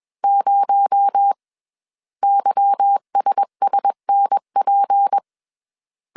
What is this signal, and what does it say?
Morse code, heard as a keyed tone, about 21 wpm, 800 Hz. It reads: 0 YSSNP
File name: CW / DET / 1 9 1500C S C 20wpm 2310